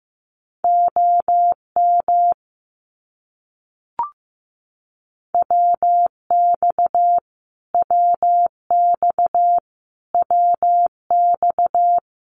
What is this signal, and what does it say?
Morse code, heard as a keyed tone, about 14 wpm, 700 Hz. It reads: OM WX WX WX
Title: Abbreviations Repeated 3x - 15wpm